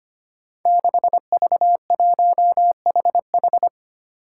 Morse code, heard as a keyed tone, about 25 wpm, 700 Hz. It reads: 6V1HH
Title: Five Challenging Letter-Number Combinations 25wpm